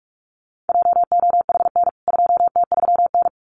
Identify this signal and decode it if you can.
Morse code, heard as a keyed tone, about 44 wpm, 700 Hz. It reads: JOHN 2T3N
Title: CWT Exchange 45wpm